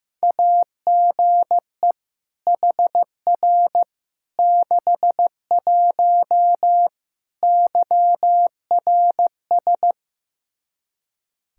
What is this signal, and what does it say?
Morse code, heard as a keyed tone, about 15 wpm, 700 Hz. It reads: AGE HR 61 YRS
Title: Practice QSO - Exchange 3 - More Info - 15wpm